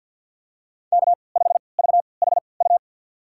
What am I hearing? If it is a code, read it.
Morse code, heard as a keyed tone, about 50 wpm, spaced as for 28 wpm, 700 Hz. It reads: K5VHU
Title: USA Call Signs - 5 Characters - 3x Character Spacing - 50wpm